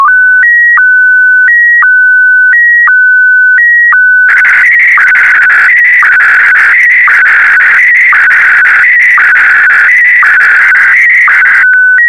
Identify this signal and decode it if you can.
SSTV (slow-scan television), Scottie DX.